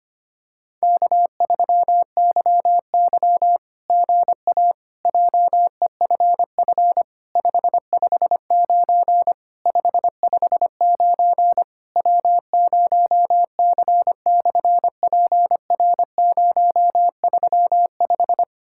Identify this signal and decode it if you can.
Morse code, heard as a keyed tone, about 24 wpm, 700 Hz. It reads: K3YY GA JEFF 559 559 W0C/PR035
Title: SOTA QSO Practice 25wpm